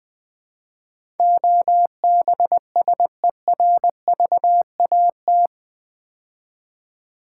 Morse code, heard as a keyed tone, about 20 wpm, 700 Hz. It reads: OBSERVAT
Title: Build A Word - 4 Letters At A Time - 11-Letter Words - 20wpm